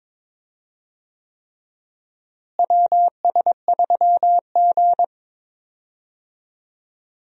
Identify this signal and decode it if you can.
Morse code, heard as a keyed tone, about 21 wpm, 700 Hz. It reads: WS3G
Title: USA Call Signs - 4 Characters - Repeated 3x - 22wpm